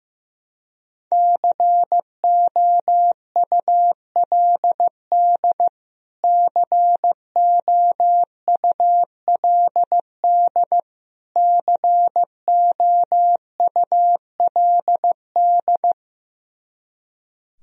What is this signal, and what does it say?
Morse code, heard as a keyed tone, about 15 wpm, 700 Hz. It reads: COULD COULD COULD
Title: Top 100 Words Repeated 15wpm